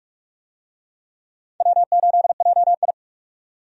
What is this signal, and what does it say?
Morse code, heard as a keyed tone, about 45 wpm, 700 Hz. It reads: W8JI
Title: USA Call Signs - 4 Characters - 45wpm